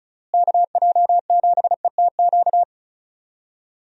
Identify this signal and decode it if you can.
Morse code, heard as a keyed tone, about 36 wpm, 700 Hz. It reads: KJ7ETQ
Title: USA Call Signs - 6 Characters - Repeated 2x - 35wpm